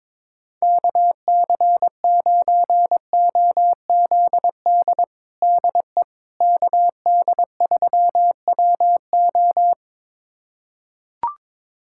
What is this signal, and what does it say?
Morse code, heard as a keyed tone, about 22 wpm, 700 Hz. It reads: KC9OZD DE KD3WO
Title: USA Callsign DE Callsign 22wpm